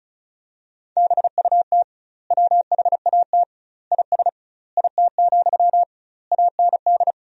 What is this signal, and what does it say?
Morse code, heard as a keyed tone, about 36 wpm, 700 Hz. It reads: BUT WHAT IS IT, AND
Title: 2x Word-Spacing - Sentences from Top 300 Words - 35wpm